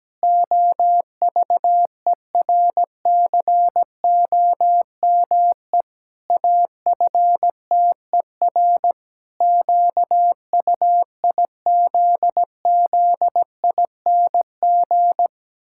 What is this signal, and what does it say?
Morse code, heard as a keyed tone, about 17 wpm, 700 Hz. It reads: OVERCOME AFTER QUIZZING
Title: Pangram Challenge v2 - 17wpm